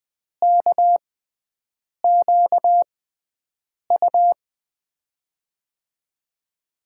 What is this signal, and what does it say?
Morse code, heard as a keyed tone, about 20 wpm, spaced as for 7 wpm, 700 Hz. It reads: KQU
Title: USA Call Signs - 5 Characters - 6x Character Spacing - 20wpm